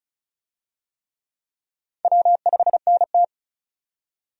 Morse code, heard as a keyed tone, about 34 wpm, 700 Hz. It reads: W5NT
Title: USA Call Signs - 4 Characters - 35wpm